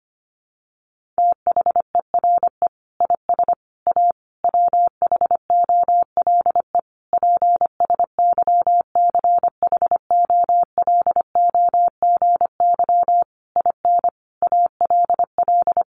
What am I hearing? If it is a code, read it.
Morse code, heard as a keyed tone, about 25 wpm, 700 Hz. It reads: THERE IS A WHOLE PSYCHOLOGY IN ALL
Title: Sentences from Top 1400 Words 25wpm